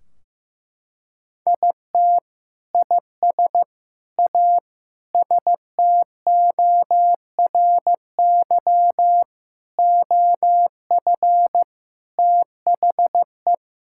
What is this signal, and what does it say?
Morse code, heard as a keyed tone, about 16 wpm, 700 Hz. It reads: IT IS A STORY OF THE
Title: Sentences from Top 1600 Words 15wpm